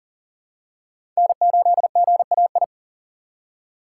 Morse code, heard as a keyed tone, about 40 wpm, 700 Hz. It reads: N8GAI
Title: USA Call Signs - 5 Characters - 3x Character Spacing - 40wpm